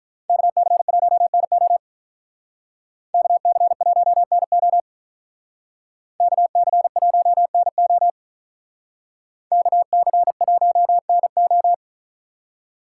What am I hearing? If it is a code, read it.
Morse code, heard as a keyed tone, about 46 wpm, 700 Hz. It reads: KC1NO KC1NO KC1NO KC1NO